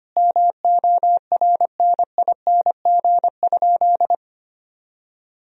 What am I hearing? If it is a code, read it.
Morse code, heard as a keyed tone, about 25 wpm, 700 Hz. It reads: MORNING?
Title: Sentences from Top 1000 Words - Review Part 3 - 25wpm